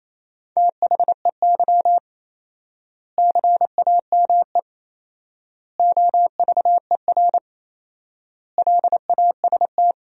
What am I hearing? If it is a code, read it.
Morse code, heard as a keyed tone, about 28 wpm, 700 Hz. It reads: THEY CAME OVER LAST
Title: 4x Word-Spacing - Sentences from Top 300 Words - 28wpm